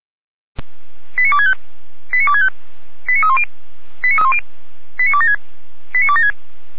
Five-tone selective call (ZVEI), heard as quick runs of five tones, tones of about 80 ms. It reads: 89265 89265 89310 89276 89276
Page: feuerwehr-orlamuende